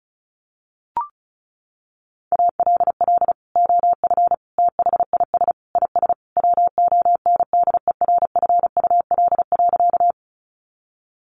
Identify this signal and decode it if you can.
Morse code, heard as a keyed tone, about 34 wpm, 700 Hz. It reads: ALL OF THIS IS WONDERFUL.
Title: Sentences from Top 1100 Words 35wpm